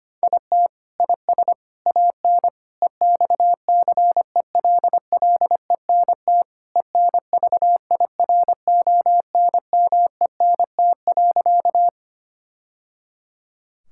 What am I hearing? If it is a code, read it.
Morse code, heard as a keyed tone, about 25 wpm, 700 Hz. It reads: IT IS AN EXCELLENT ENVIRONMENT.
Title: Sentences from Top 1000 Words - Review Part 5 - 25wpm